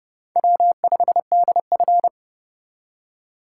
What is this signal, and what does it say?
Morse code, heard as a keyed tone, about 29 wpm, 700 Hz. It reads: W5DF
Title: USA Call Signs - 4 Characters - 3x Character Spacing - 30wpm